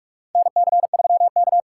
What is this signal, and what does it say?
Morse code, heard as a keyed tone, about 46 wpm, 700 Hz. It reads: NC3K
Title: USA Call Signs - 4 Characters - 6x Character Spacing - 45wpm